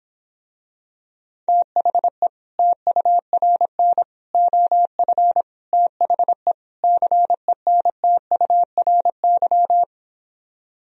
Morse code, heard as a keyed tone, about 26 wpm, 700 Hz. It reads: THE TURN OF THE CENTURY